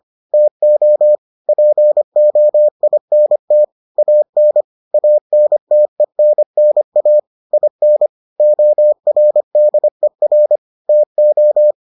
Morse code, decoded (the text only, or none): TO POINT AN ANTENNA IN ORDER TO